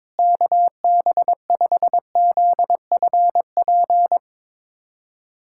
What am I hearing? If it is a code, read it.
Morse code, heard as a keyed tone, about 21 wpm, 700 Hz. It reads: KB5ZFP
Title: USA Callsign DE Callsign - 6 Characters - 22wpm